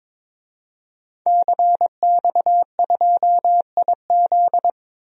Morse code, heard as a keyed tone, about 21 wpm, 700 Hz. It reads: CX2IZ
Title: DXCC-100 Call Signs - 5 Characters - Repeated 2x - 22wpm